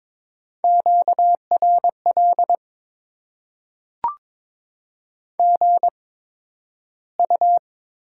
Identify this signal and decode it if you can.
Morse code, heard as a keyed tone, about 22 wpm, 700 Hz. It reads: Q R L G U
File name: Top 20 QSO Elements - 8x Character Spacing - 22wpm